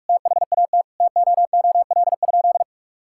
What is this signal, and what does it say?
Morse code, heard as a keyed tone, about 45 wpm, 700 Hz. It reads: THAT TOOL?
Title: Sentences from Top 1000 Words - Review Part 6 - 45wpm